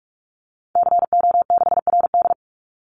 Morse code, heard as a keyed tone, about 44 wpm, 700 Hz. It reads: CO6RD
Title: Call Signs Repeated 2x - 45wpm